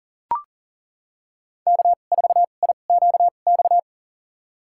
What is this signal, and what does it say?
Morse code, heard as a keyed tone, about 40 wpm, spaced as for 29 wpm, 700 Hz. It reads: K4IQX
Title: USA Call Signs - 5 Characters - 2x Character Spacing - 40wpm